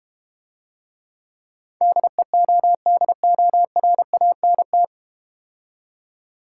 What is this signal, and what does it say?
Morse code, heard as a keyed tone, about 32 wpm, 700 Hz. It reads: DEODORANT